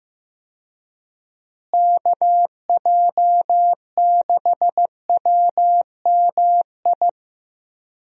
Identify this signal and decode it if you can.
Morse code, heard as a keyed tone, about 15 wpm, 700 Hz. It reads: KJ6WMI
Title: USA Call Signs - 6 Characters - Repeated 3x - 15wpm